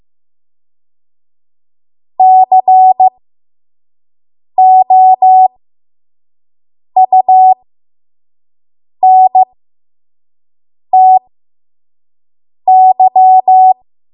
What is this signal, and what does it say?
Morse code, heard as a keyed tone, about 15 wpm, spaced as for 5 wpm, 750 Hz. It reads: COUNTY
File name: Morse / 05WPM